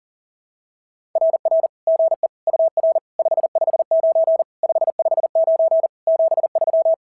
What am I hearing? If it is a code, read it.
Morse code, heard as a keyed tone, about 40 wpm, 660 Hz. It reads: RR GE UR 559 559 73
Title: SOTA QSO Practice 40wpm